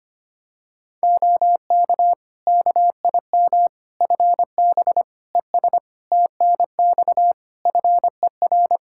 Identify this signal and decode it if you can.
Morse code, heard as a keyed tone, about 25 wpm, 700 Hz. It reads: OK KIM FB ES TNX FER
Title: Practice QSO - Exchange 2 - Info - 25wpm